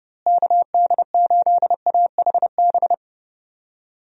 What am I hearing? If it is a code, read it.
Morse code, heard as a keyed tone, about 32 wpm, 700 Hz. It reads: KD8AHB
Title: USA Call Signs - 6 Characters - Repeated 2x - 30wpm